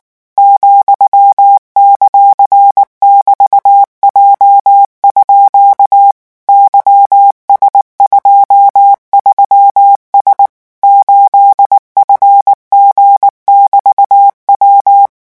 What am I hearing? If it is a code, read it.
Morse code, heard as a keyed tone, about 19 wpm, 800 Hz. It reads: ,;<BT>J_ YS23S 8FG<BT>W
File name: morse mp3 / revision15-vitesse 18 mots